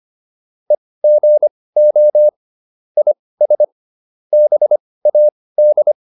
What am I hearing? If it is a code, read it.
Morse code, heard as a keyed tone, about 25 wpm, spaced as for 18 wpm, 600 Hz. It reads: EGO IS BAD
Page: intermediate-practice-files